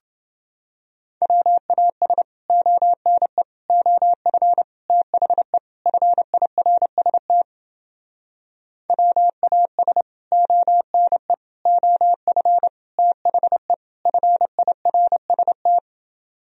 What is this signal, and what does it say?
Morse code, heard as a keyed tone, about 30 wpm, 700 Hz. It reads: WAS ONE OF THE FIRST WAS ONE OF THE FIRST